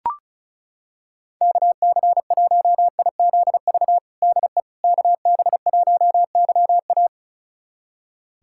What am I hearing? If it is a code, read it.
Morse code, heard as a keyed tone, about 36 wpm, 700 Hz. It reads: KC1IZV DE KB1YA
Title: USA Callsign DE Callsign 35wpm